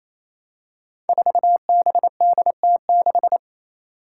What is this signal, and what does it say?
Morse code, heard as a keyed tone, about 28 wpm, 700 Hz. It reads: 4BDT6
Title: Five Challenging Letter-Number Combinations 28wpm